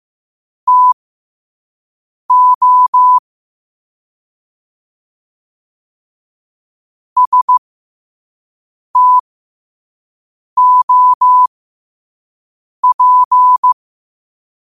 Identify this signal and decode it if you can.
Morse code, heard as a keyed tone, about 15 wpm, spaced as for 5 wpm, 1000 Hz. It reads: TO STOP
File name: qotdfiles / 5wpm